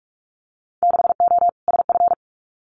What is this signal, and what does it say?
Morse code, heard as a keyed tone, about 46 wpm, 700 Hz. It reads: 6O SF
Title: Winter Field Day Exchanges Encore 45wpm